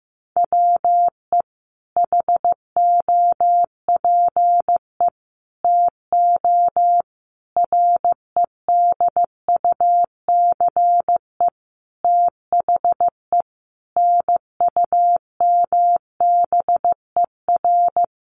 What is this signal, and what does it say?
Morse code, heard as a keyed tone, about 15 wpm, 700 Hz. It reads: WE HOPE TO REDUCE THE NUMBER
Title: Sentences from Top 1100 Words 15wpm